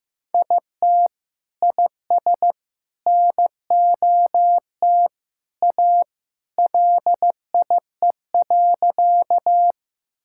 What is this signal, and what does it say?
Morse code, heard as a keyed tone, about 15 wpm, 700 Hz. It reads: IT IS NOT A LIE.
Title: Sentences from Top 1600 Words 15wpm